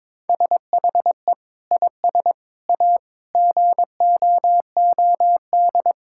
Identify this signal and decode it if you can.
Morse code, heard as a keyed tone, about 22 wpm, 700 Hz. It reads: SHE IS A GOOD
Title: Sentences from Top 1600 Words 22wpm